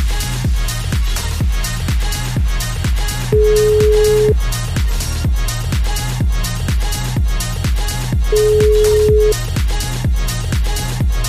Ringback tone (Europe, 425 Hz).